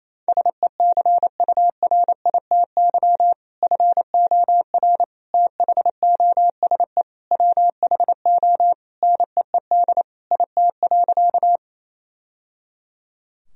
Morse code, heard as a keyed tone, about 28 wpm, 700 Hz. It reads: SECURITY FOR THOSE WHO NEED IT.
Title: Sentences from Top 1000 Words - Review Part 4 - 28wpm